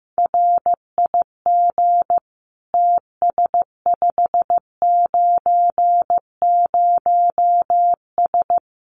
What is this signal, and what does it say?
Morse code, heard as a keyed tone, about 15 wpm, 700 Hz. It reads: RIG TS590S
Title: Common QSO Phrases Encore 15wpm